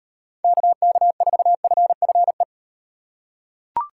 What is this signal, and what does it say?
Morse code, heard as a keyed tone, about 35 wpm, 700 Hz. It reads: KK4FFE